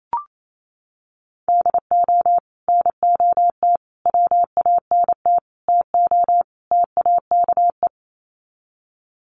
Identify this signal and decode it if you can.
Morse code, heard as a keyed tone, about 28 wpm, 700 Hz. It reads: DO NOT WANT TO TAKE
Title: Sets of 5 Words Encore2 28wpm